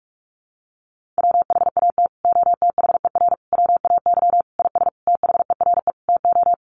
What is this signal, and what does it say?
Morse code, heard as a keyed tone, about 44 wpm, 700 Hz. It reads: WHAT OTHER WAY IS THERE TO
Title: Sentences from Top 100 Words 45wpm